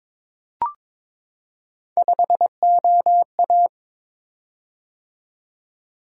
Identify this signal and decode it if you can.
Morse code, heard as a keyed tone, about 22 wpm, 700 Hz. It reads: 5OA
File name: practice / 22wpm / Three Letter-Number Combinations - 22wpm